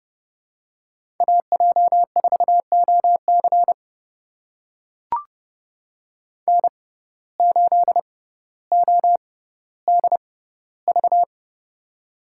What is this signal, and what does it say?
Morse code, heard as a keyed tone, about 32 wpm, spaced as for 16 wpm, 700 Hz. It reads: AJ4OC N8ODV
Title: USA Call Signs - 5 Characters - 6x Character Spacing - 30wpm